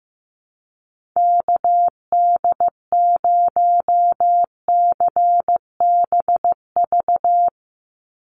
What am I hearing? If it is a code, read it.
Morse code, heard as a keyed tone, about 15 wpm, 700 Hz. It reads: KD0CBV
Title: USA Call Signs 15wpm